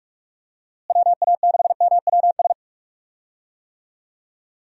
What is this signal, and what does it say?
Morse code, heard as a keyed tone, about 46 wpm, 700 Hz. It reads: WA6MWS
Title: USA Call Signs - 6 Characters - Repeated 3x - 45wpm